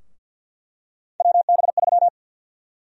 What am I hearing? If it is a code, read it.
Morse code, heard as a keyed tone, about 50 wpm, 700 Hz. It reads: WB3
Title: USA Call Sign Prefixes - Repeated 2x - 50wpm